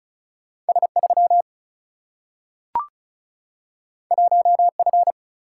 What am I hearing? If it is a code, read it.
Morse code, heard as a keyed tone, about 36 wpm, 700 Hz. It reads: S3 1F
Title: Single-Letter Single-Number 35wpm